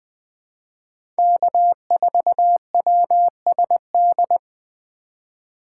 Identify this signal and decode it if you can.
Morse code, heard as a keyed tone, about 20 wpm, 700 Hz. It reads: K4WSD
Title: 2023 POTA Activator Call Signs - 20wpm